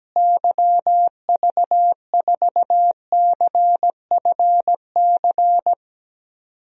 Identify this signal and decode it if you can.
Morse code, heard as a keyed tone, about 17 wpm, 700 Hz. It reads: YV4CFC
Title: DXCC-100 Call Signs - 6 Characters - Repeated 2x - 17wpm